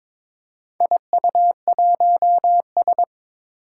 Morse code, heard as a keyed tone, about 22 wpm, 700 Hz. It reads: IU1S
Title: Four Challenging Letter-Number Combinations 22wpm